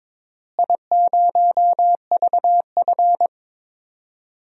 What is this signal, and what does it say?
Morse code, heard as a keyed tone, about 22 wpm, 700 Hz. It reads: I0VF
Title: DXCC-100 Call Signs - 4 Characters - Repeated 3x - 22wpm